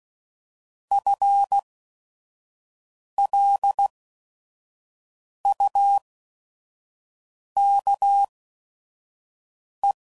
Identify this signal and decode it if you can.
Morse code, heard as a keyed tone, about 16 wpm, spaced as for 5 wpm, 800 Hz. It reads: FLUKE